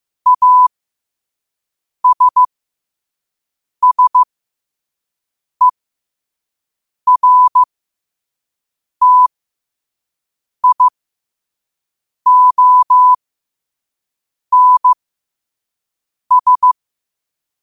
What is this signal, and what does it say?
Morse code, heard as a keyed tone, about 15 wpm, spaced as for 5 wpm, 1000 Hz. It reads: ASSERTIONS
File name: qotdfiles / 5wpm